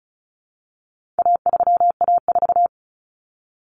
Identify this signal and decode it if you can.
Morse code, heard as a keyed tone, about 34 wpm, 700 Hz. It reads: A3A4
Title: Common Cut Numbers - 4-Digit 35wpm